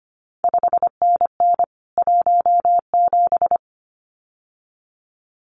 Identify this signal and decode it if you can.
Morse code, heard as a keyed tone, about 25 wpm, 700 Hz. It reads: 5NN 17
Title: CQ WW DX Contest Exchange Encore - 25wpm